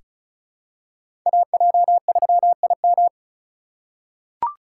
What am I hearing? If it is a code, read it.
Morse code, heard as a keyed tone, about 36 wpm, 700 Hz. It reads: AJ3IM
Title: USA Call Signs - 5 Characters - 4x Character Spacing - 35wpm